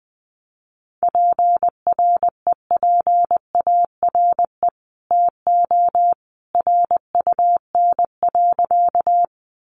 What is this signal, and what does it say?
Morse code, heard as a keyed tone, about 20 wpm, 700 Hz. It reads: PREPARE TO RUN.
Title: Sentences from Top 900 Words 20wpm